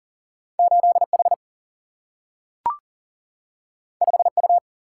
Morse code, heard as a keyed tone, about 40 wpm, 700 Hz. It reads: 8H 5U
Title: Single-Letter Single-Number 40wpm